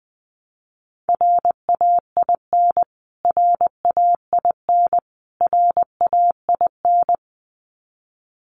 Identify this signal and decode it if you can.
Morse code, heard as a keyed tone, about 20 wpm, 700 Hz. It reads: RAIN RAIN RAIN
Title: Top 300 Words Repeated 20wpm